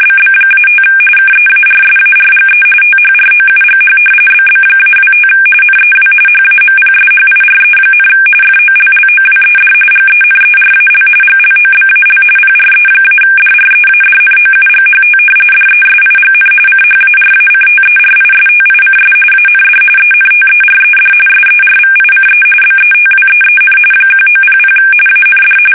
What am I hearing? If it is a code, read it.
RTTY, about 45.45 baud. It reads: BB-5 KEARSARGE THROUGH BB-25 NEW HAMPSHIRE, FOLLOWED GENERAL GLOBAL PRE-DREADNOUGHT DESIGN CHARACTERISTICS AND ENTERED SERVICE BETWEEN 1900 AND
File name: fsk850 / battleships